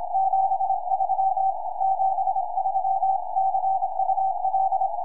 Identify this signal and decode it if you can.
Morse code, heard as a keyed tone, about 25 wpm, 770 Hz. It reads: 73 73 CUL
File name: ei3io 4m